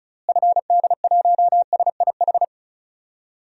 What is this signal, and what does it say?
Morse code, heard as a keyed tone, about 34 wpm, 700 Hz. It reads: FD1SIH
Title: DXCC-100 Call Signs - 6 Characters - Repeated 3x - 35wpm